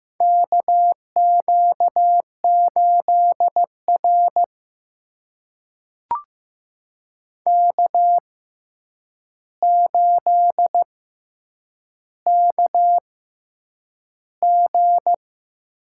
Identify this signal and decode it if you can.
Morse code, heard as a keyed tone, about 15 wpm, spaced as for 8 wpm, 700 Hz. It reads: KQ8R K8KG
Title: USA Call Signs - 4 Characters - 6x Character Spacing - 15wpm